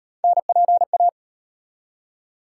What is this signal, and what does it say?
Morse code, heard as a keyed tone, about 36 wpm, 700 Hz. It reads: NPA